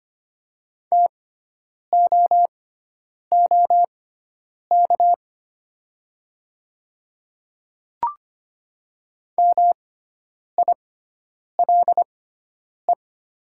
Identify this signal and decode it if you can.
Morse code, heard as a keyed tone, about 25 wpm, spaced as for 9 wpm, 700 Hz. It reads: TOOK MILE
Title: 6x Character Spacing - Top 500 Words - 25wpm